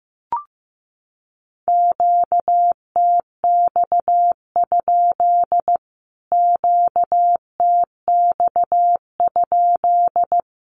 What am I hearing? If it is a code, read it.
Morse code, heard as a keyed tone, about 15 wpm, 700 Hz. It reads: QTX? QTX?
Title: Q-Codes Repeated 3x 15wpm